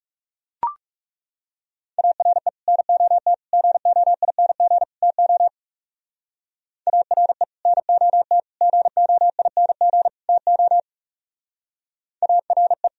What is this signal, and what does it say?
Morse code, heard as a keyed tone, about 36 wpm, 700 Hz. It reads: ARE NOT GOING TO ARE NOT GOING TO ARE
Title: Speed-Racing - Sets of 4 Words 30wpm